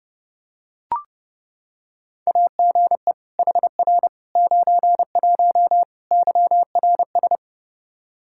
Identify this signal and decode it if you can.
Morse code, heard as a keyed tone, about 32 wpm, 700 Hz. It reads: AGE HR 91 YRS
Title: Practice QSO - Exchange 3 - More Info - 30wpm